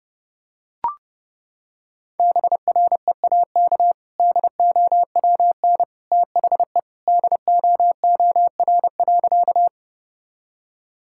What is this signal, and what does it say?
Morse code, heard as a keyed tone, about 30 wpm, 700 Hz. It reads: BREAK DOWN THE DOOR.
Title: Sentences from Top 1000 Words - Review Part 6 - 30wpm